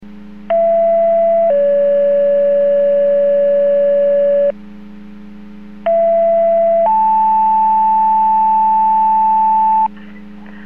Two-tone page, heard as a one-second tone, then a three-second tone, about 670 then 570 Hz, about 670 then 880 Hz. It is heard from 0.5 to 9.9 s.
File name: Portage 154.385 Paging